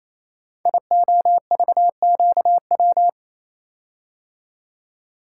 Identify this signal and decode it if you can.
Morse code, heard as a keyed tone, about 27 wpm, 700 Hz. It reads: IOVQW
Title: Code Groups - 5 Letter:Numbers - 28wpm